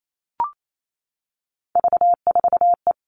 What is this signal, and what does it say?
Morse code, heard as a keyed tone, about 28 wpm, 700 Hz. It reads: V4E
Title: Three Challenging Letter-Number Combinations 28wpm